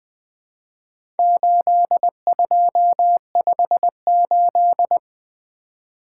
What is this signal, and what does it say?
Morse code, heard as a keyed tone, about 20 wpm, 700 Hz. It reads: 8258
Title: Four Letter-Number Combinations - 20wpm